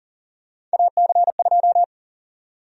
Morse code, heard as a keyed tone, about 38 wpm, 700 Hz. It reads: AC2
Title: USA Call Sign Prefixes - Repeated 2x - 40wpm